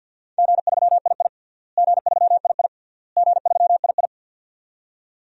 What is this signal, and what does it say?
Morse code, heard as a keyed tone, about 51 wpm, 700 Hz. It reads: G3II G3II G3II
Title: DXCC-100 Call Signs - 4 Characters - Repeated 3x - 50wpm